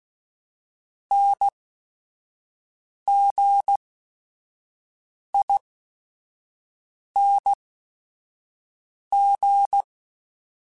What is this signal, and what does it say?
Morse code, heard as a keyed tone, about 16 wpm, spaced as for 5 wpm, 800 Hz. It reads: NGING